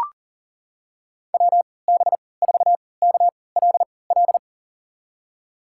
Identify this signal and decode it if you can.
Morse code, heard as a keyed tone, about 37 wpm, spaced as for 22 wpm, 700 Hz. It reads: WB4KLL
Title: USA Call Signs - 6 Characters - 3x Character Spacing - 40wpm